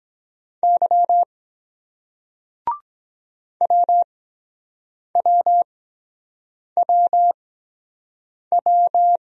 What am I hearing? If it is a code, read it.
Morse code, heard as a keyed tone, about 23 wpm, spaced as for 7 wpm, 700 Hz. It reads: Y WWWW